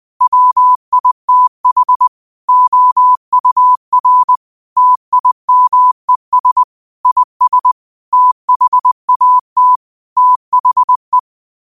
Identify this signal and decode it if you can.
Morse code, heard as a keyed tone, about 20 wpm, 1000 Hz. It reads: WITH OUR TIMES IS THAT THE
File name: 20wpm